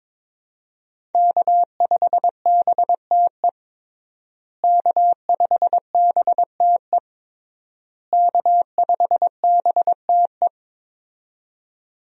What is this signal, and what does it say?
Morse code, heard as a keyed tone, about 22 wpm, 700 Hz. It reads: K5BTE K5BTE K5BTE
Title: USA Call Signs - 5 Characters - Repeated 3x - 22wpm